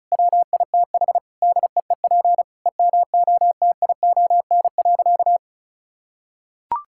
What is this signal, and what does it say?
Morse code, heard as a keyed tone, about 36 wpm, 700 Hz. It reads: WITH DEEP EMOTION.
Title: Sentences from Top 1600 Words 35wpm